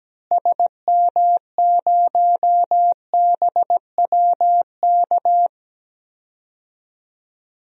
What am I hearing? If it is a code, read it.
Morse code, heard as a keyed tone, about 17 wpm, 700 Hz. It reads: SM0BWK
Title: DXCC-100 Call Signs - 6 Characters - Repeated 3x - 17wpm